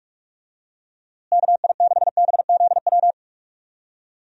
Morse code, heard as a keyed tone, about 45 wpm, 700 Hz. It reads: KI6BZW
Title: USA Call Signs - 6 Characters - Repeated 3x - 45wpm